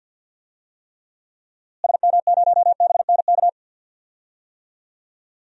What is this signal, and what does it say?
Morse code, heard as a keyed tone, about 46 wpm, 700 Hz. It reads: SM0BNK
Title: DXCC-100 Call Signs - 6 Characters - Repeated 3x - 50wpm